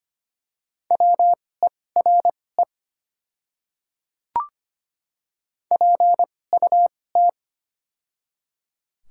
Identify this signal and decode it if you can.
Morse code, heard as a keyed tone, about 25 wpm, spaced as for 18 wpm, 700 Hz. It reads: WERE PUT
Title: 2x Character Spacing - Top 200 Words - 25wpm